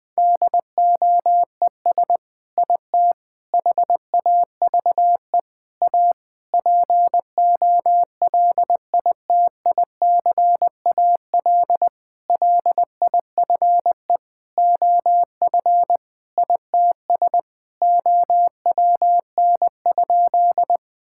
Morse code, heard as a keyed tone, about 20 wpm, 700 Hz. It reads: DOES IT HAVE A POLITICAL LIFE OF ITS OWN?
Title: Sentences from Top 1000 Words - Review Part 5 - 20wpm